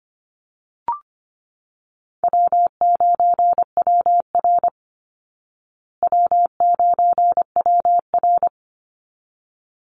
Morse code, heard as a keyed tone, about 25 wpm, 700 Hz. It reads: W9WR W9WR
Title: Easy Call Signs Repeated 2x - 25wpm